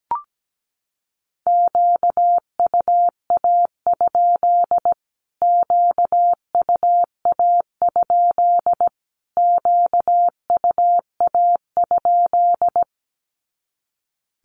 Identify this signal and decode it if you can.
Morse code, heard as a keyed tone, about 17 wpm, 700 Hz. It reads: QUA? QUA? QUA?
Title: Q-Codes Repeated 3x 17wpm